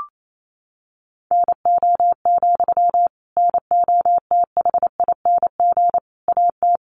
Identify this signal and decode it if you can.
Morse code, heard as a keyed tone, about 28 wpm, 700 Hz. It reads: NO, NOTHING AT
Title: Sentences from Top 500 Words 28wpm